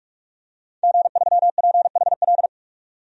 Morse code, heard as a keyed tone, about 46 wpm, 700 Hz. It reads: G3PHL
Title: DXCC-100 Call Signs - 5 Characters - 45wpm